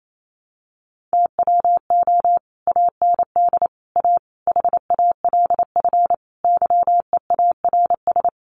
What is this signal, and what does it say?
Morse code, heard as a keyed tone, about 28 wpm, 700 Hz. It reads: TWO AND A HALF YEARS
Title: Sets of 5 Words 28wpm